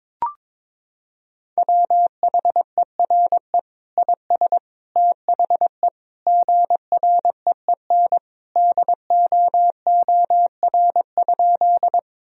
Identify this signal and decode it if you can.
Morse code, heard as a keyed tone, about 22 wpm, 700 Hz. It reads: WHERE IS THE GREEN DOOR?
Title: Sentences from Top 1000 Words - Review Part 2 - 22wpm